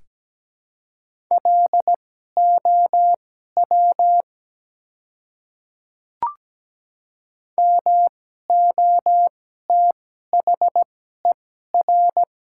Morse code, heard as a keyed tone, about 17 wpm, spaced as for 12 wpm, 700 Hz. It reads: LOW MOTHER
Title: 2x Character Spacing - Top 200 Words - 17wpm